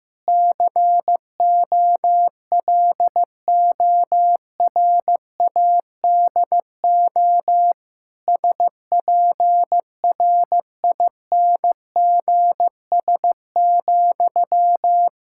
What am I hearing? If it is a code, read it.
Morse code, heard as a keyed tone, about 16 wpm, 700 Hz. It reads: COLORADO SPRINGS,
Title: Practice QSO - Exchange 1 - Beginning - 15wpm